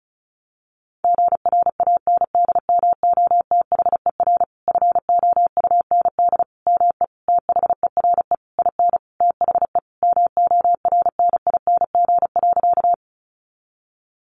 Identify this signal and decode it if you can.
Morse code, heard as a keyed tone, about 34 wpm, 700 Hz. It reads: GRANDMOTHER FOUND ME THERE IN THE MORNING.
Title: Sentences from Top 1200 Words 35wpm